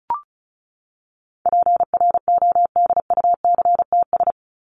Morse code, heard as a keyed tone, about 34 wpm, 700 Hz. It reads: PRODUCTS
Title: ES:S Suffix - Top 100 Words with Base Words 35wpm